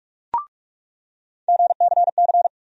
Morse code, heard as a keyed tone, about 45 wpm, 700 Hz. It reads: GCC
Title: USA Call Sign Postfixes - 3 Letters 45wpm